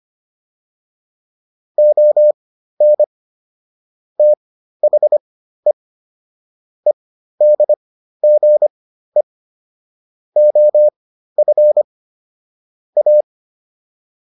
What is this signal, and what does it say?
Morse code, heard as a keyed tone, about 25 wpm, spaced as for 13 wpm, 600 Hz. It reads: ON THE EDGE OF A